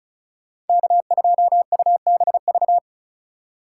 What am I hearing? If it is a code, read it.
Morse code, heard as a keyed tone, about 34 wpm, 700 Hz. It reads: K2UBV
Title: USA Call Signs - 5 Characters - 4x Character Spacing - 35wpm